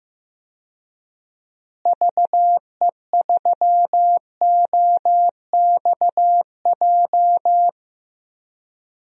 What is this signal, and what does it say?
Morse code, heard as a keyed tone, about 15 wpm, 700 Hz. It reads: VE3OXJ
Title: DXCC-100 Call Signs - 6 Characters - Repeated 3x - 15wpm